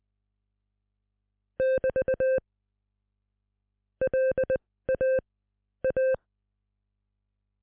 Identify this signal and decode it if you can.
Morse code, heard as a keyed tone, about 20 wpm, spaced as for 15 wpm, 520 Hz. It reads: <BT> LA A